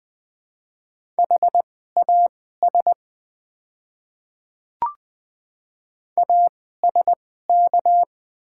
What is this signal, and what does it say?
Morse code, heard as a keyed tone, about 20 wpm, spaced as for 15 wpm, 700 Hz. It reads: HAS ASK
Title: 2x Character Spacing - Top 200 Words - 20wpm